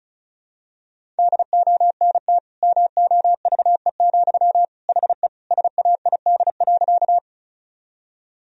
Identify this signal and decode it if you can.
Morse code, heard as a keyed tone, about 36 wpm, 700 Hz. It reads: DONT MOVE, HE SAID.
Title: Sentences from Top 1000 Words - Review Part 1 - 35wpm